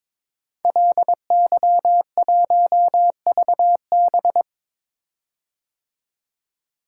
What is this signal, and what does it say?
Morse code, heard as a keyed tone, about 22 wpm, 700 Hz. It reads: LY1VB
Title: Code Groups - 5 Letter:Numbers - 22wpm